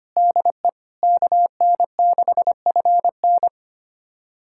Morse code, heard as a keyed tone, about 25 wpm, 700 Hz. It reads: DE KN6FN
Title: USA Callsign DE Callsign - 5 Characters - 25wpm